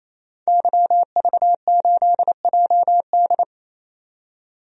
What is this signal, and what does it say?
Morse code, heard as a keyed tone, about 27 wpm, 700 Hz. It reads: YV8JD
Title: DXCC-100 Call Signs - 5 Characters - 28wpm